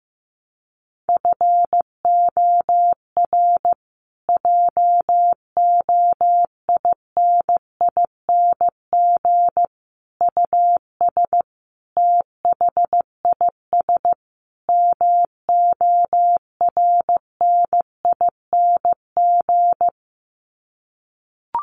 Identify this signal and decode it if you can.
Morse code, heard as a keyed tone, about 15 wpm, 700 Hz. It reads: FOR JOINING US THIS MORNING
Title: Sets of 5 Words Encore 15wpm